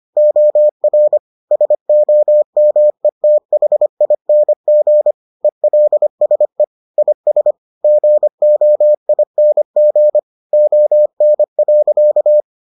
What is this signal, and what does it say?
Morse code, heard as a keyed tone, about 25 wpm, 600 Hz. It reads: OR SOMETHING ELSE IS GOING ON.